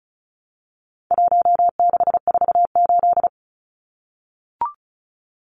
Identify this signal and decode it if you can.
Morse code, heard as a keyed tone, about 34 wpm, 700 Hz. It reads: 1648
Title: Numbers - 4-Digit 35wpm